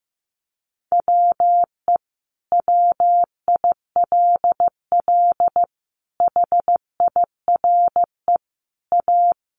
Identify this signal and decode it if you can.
Morse code, heard as a keyed tone, about 15 wpm, 700 Hz. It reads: WE WILL HIRE A
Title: Sentences from Top 1400 Words 15wpm